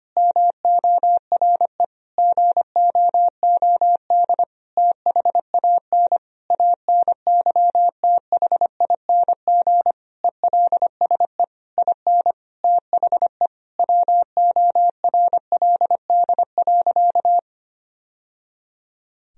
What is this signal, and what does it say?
Morse code, heard as a keyed tone, about 25 wpm, 700 Hz. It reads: MORE GOOD THAN ANYTHING ELSE IN THE WORLD.
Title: Sentences from Top 1000 Words - Review Part 4 - 25wpm